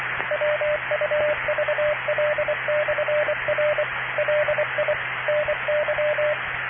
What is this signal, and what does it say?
Morse code, heard as a keyed tone, about 23 wpm, 600 Hz, with radio noise. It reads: WUVL/R LI NY